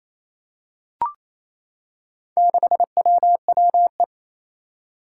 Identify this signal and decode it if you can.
Morse code, heard as a keyed tone, about 27 wpm, 700 Hz. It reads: 6WWE
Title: Four Challenging Letter-Number Combinations 28wpm